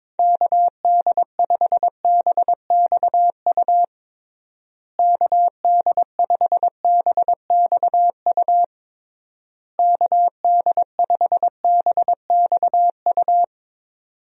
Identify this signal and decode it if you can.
Morse code, heard as a keyed tone, about 22 wpm, 700 Hz. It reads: KD5BXU KD5BXU KD5BXU
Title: USA Call Signs - 6 Characters - Repeated 3x - 22wpm